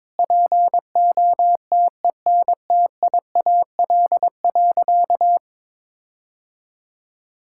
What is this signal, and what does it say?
Morse code, heard as a keyed tone, about 22 wpm, 700 Hz. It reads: POTENTIAL.
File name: Sentences with 9-Word Phrases - 22wpm